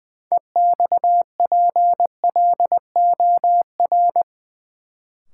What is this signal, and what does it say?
Morse code, heard as a keyed tone, about 20 wpm, 700 Hz. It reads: EXPLOR
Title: Sentences from Top 1100 Words - New Word Drill Alt - 20wpm